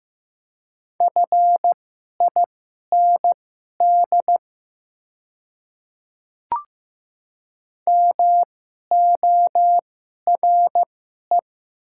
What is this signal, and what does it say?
Morse code, heard as a keyed tone, about 15 wpm, spaced as for 11 wpm, 700 Hz. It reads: FIND MORE
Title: 2x Character Spacing - Top 100 Words - 15wpm